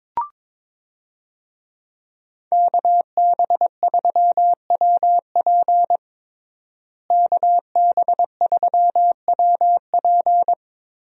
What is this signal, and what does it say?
Morse code, heard as a keyed tone, about 21 wpm, 700 Hz. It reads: KB3WP KB3WP
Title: USA Call Signs - 5 Characters - Repeated 2x - 22wpm